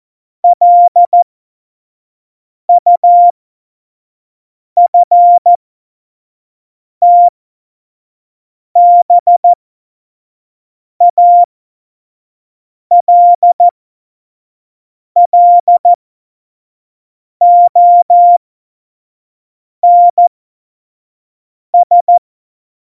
Morse code, heard as a keyed tone, about 14 wpm, spaced as for 5 wpm, 700 Hz. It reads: LUFTBALLONS
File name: Morsekurs / Witze / Witz 26. September 17, Tempo 25